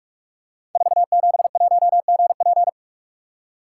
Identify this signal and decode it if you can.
Morse code, heard as a keyed tone, about 44 wpm, 700 Hz. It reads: 471GP
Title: Code Groups - 5 Letter:Numbers - 45wpm